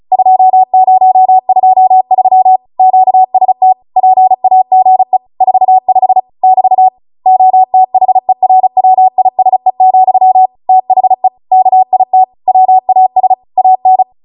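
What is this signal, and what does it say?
Morse code, heard as a keyed tone, about 36 wpm, 750 Hz. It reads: 2023 QST PAGE 45 <BT> OTHERWISE, THE KIT WAS AN
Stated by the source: Amateur radio Morse code practice archive transmission at 35 wpm.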